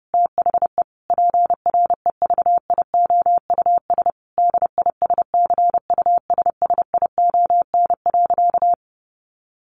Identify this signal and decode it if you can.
Morse code, heard as a keyed tone, about 30 wpm, 700 Hz. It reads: THE PREVIOUS DISCUSSION.
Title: Sentences from Top 1500 Words 30wpm